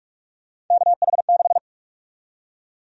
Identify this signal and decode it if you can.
Morse code, heard as a keyed tone, about 46 wpm, 700 Hz. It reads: KH6
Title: USA Call Sign Prefixes - Repeated 2x - 45wpm